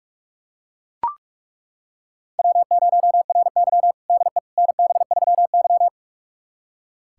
Morse code, heard as a keyed tone, about 41 wpm, 700 Hz. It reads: W0RY DE NB3Y
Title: USA Callsign DE Callsign - 4 Characters - 45wpm